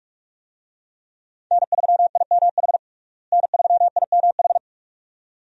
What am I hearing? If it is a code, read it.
Morse code, heard as a keyed tone, about 46 wpm, 700 Hz. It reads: N3IMH N3IMH
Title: USA Call Signs - 5 Characters - Repeated 2x - 45wpm